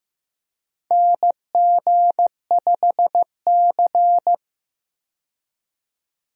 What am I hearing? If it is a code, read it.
Morse code, heard as a keyed tone, about 15 wpm, 700 Hz. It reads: NG5C
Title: USA Call Signs - 4 Characters - 15wpm